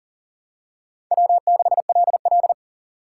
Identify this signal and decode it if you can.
Morse code, heard as a keyed tone, about 40 wpm, 700 Hz. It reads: W6LL
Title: USA Call Signs - 4 Characters - 6x Character Spacing - 40wpm